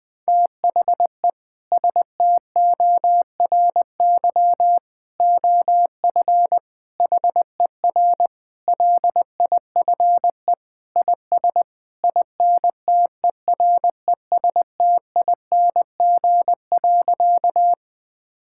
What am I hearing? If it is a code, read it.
Morse code, heard as a keyed tone, about 21 wpm, 700 Hz. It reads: THE STORY OF HER LIFE IS INTERESTING.
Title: Sentences with 3-Word Phrases - 20wpm